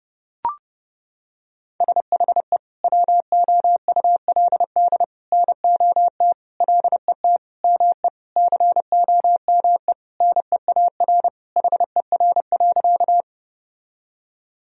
Morse code, heard as a keyed tone, about 30 wpm, 700 Hz. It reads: SHE WOULD NOT LET ME COME NEAR HER.
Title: Sentences from Top 1000 Words - Review Part 1 - 30wpm